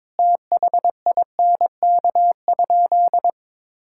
Morse code, heard as a keyed tone, about 22 wpm, 700 Hz. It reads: THINK?
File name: practice / 22wpm / Sentences with 3-Word Phrases - 22wpm